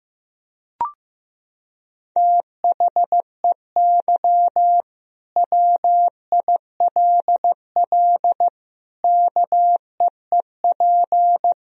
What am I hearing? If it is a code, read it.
Morse code, heard as a keyed tone, about 15 wpm, 700 Hz. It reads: THEY WILL KEEP
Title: Sentences from Top 1000 Words - Review Part 3 - 15wpm